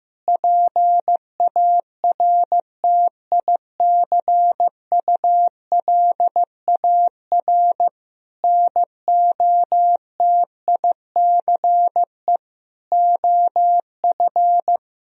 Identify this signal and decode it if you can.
Morse code, heard as a keyed tone, about 15 wpm, 700 Hz. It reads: PARTICULAR NOTICE OF
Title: Sentences from Top 1000 Words - Review Part 4 - 15wpm